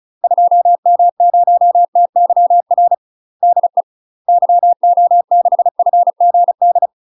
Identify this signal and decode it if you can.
Morse code, heard as a keyed tone, about 35 wpm, 700 Hz. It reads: 2M0TYR DE YO6FGD
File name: UK MASTER QSOs TRAINING / 35WPM / UK MASTER QSOs 0005